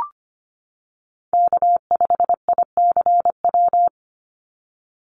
Morse code, heard as a keyed tone, about 25 wpm, 700 Hz. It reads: K5ICW
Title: USA Call Signs 25wpm